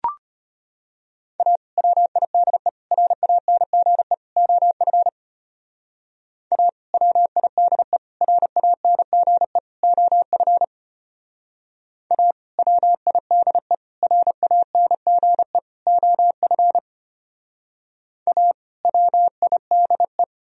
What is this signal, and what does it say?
Morse code, heard as a keyed tone, about 33 wpm, 700 Hz. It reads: A WIDE RANGE OF A WIDE RANGE OF A WIDE RANGE OF A WIDE